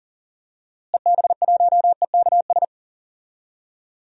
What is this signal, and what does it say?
Morse code, heard as a keyed tone, about 40 wpm, 700 Hz. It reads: EB1EKS
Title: DXCC-100 Call Signs - 6 Characters - Repeated 2x - 40wpm